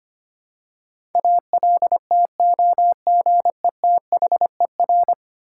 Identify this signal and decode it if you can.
Morse code, heard as a keyed tone, about 25 wpm, 700 Hz. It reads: ALTOGETHER
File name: practice / 25wpm / Ten Letter Words - 25wpm